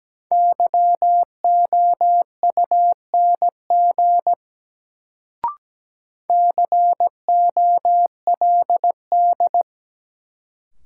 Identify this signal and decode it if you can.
Morse code, heard as a keyed tone, about 17 wpm, 700 Hz. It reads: YOUNG COLD
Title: Sentences from Top 400 Words - New Word Drill - 17wpm